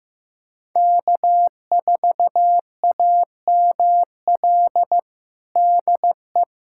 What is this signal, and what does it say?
Morse code, heard as a keyed tone, about 15 wpm, 700 Hz. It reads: K4AML DE
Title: USA Callsign DE Callsign - 5 Characters - 15wpm